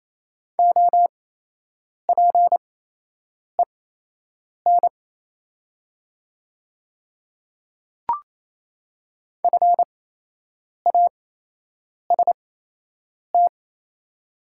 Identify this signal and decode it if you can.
Morse code, heard as a keyed tone, about 28 wpm, spaced as for 8 wpm, 690 Hz. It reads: OPEN FAST
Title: 8x Character Spacing - Top 500 Words - 28wpm